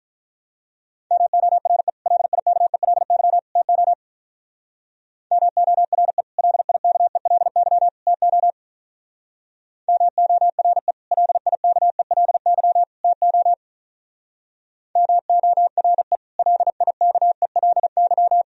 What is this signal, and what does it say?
Morse code, heard as a keyed tone, about 42 wpm, 700 Hz. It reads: MORE LIKELY TO MORE LIKELY TO MORE LIKELY TO MORE LIKELY
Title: Speed-Racing - Sets of 3 Words 35wpm